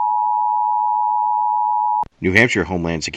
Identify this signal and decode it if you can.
Emergency Alert System (SAME) tones (the 853 and 960 Hz attention tone), with speech. It is heard from 0.0 to 2.1 s.